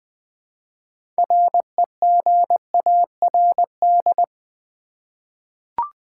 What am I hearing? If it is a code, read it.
Morse code, heard as a keyed tone, about 20 wpm, 700 Hz. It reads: REGARD
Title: 6 Letter Words - First 3 Letters Given - 20wpm